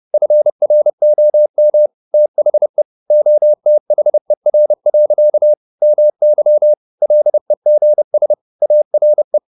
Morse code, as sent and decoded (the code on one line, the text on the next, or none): ..-. .-. --- -- / - .... / --- - .... .-. .-.-.- / -- -.-- / .-.. --. ... / .- .-. .
FROM THE OTHER. MY LEGS ARE